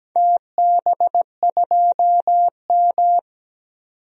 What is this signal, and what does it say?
Morse code, heard as a keyed tone, about 17 wpm, 700 Hz. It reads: TB2M
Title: DXCC-100 Call Signs - 4 Characters - 17wpm